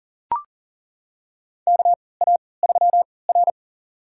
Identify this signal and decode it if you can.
Morse code, heard as a keyed tone, about 40 wpm, spaced as for 23 wpm, 700 Hz. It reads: KA3R
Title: USA Call Signs - 4 Characters - 3x Character Spacing - 40wpm